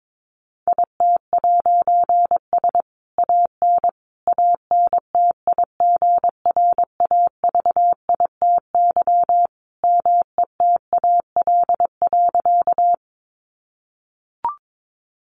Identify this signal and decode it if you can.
Morse code, heard as a keyed tone, about 22 wpm, 700 Hz. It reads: IT'S AN ANTIGRAVITY METAL.
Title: A Sound Of Thunder 22wpm